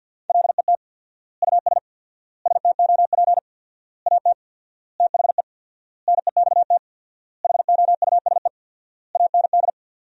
Morse code, heard as a keyed tone, about 50 wpm, 700 Hz. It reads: LET US STOP AT THE NEXT HOUSE AND
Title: 4x Word-Spacing - Sentences from Top 400 Words - 50wpm